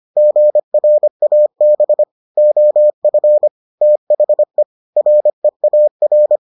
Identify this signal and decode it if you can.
Morse code, heard as a keyed tone, about 25 wpm, 600 Hz. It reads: GRAB OF THE REAR